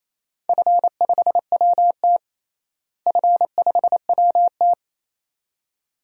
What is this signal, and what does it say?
Morse code, heard as a keyed tone, about 28 wpm, 700 Hz. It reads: F5WT F5WT
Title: DXCC-100 Call Signs - 4 Characters - Repeated 2x - 28wpm